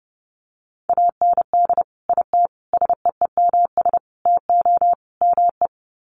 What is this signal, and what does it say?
Morse code, heard as a keyed tone, about 30 wpm, 700 Hz. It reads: AND IT SEEMS TO ME
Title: Sets of 5 Words Encore2 30wpm